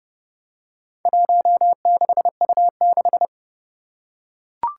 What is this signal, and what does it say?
Morse code, heard as a keyed tone, about 30 wpm, 700 Hz. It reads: 16U6
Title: Four Challenging Letter-Number Combinations 30wpm